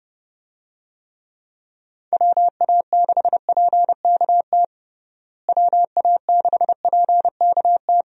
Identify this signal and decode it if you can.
Morse code, heard as a keyed tone, about 30 wpm, 700 Hz. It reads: WA6PKT WA6PKT
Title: USA Call Signs - 6 Characters - Repeated 2x - 30wpm